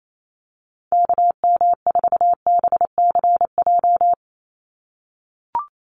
Morse code, heard as a keyed tone, about 28 wpm, 700 Hz. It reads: KM4BCJ
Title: USA Call Signs 28wpm